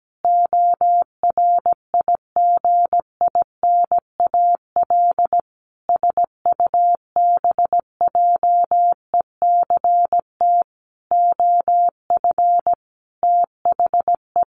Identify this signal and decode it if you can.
Morse code, heard as a keyed tone, about 17 wpm, 700 Hz. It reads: ORIGINAL SUBJECT OF THE
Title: Sentences from Top 900 Words 17wpm